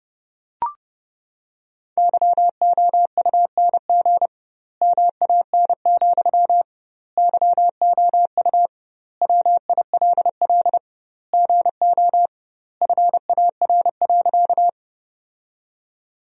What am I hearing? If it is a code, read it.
Morse code, heard as a keyed tone, about 30 wpm, 700 Hz. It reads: YOUNG MAN, YOU WILL GO FAR.
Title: 2x Word-Spacing - Sentences from Top 400 Words - 30wpm